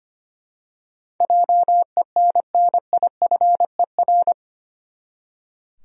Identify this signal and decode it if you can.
Morse code, heard as a keyed tone, about 25 wpm, 700 Hz. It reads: JENNIFER
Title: USA Common Names - 25wpm